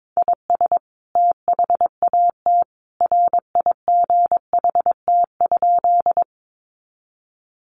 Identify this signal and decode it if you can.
Morse code, heard as a keyed tone, about 22 wpm, 700 Hz. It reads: IS THAT RIGHT?
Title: Sentences from Top 200 Words 22wpm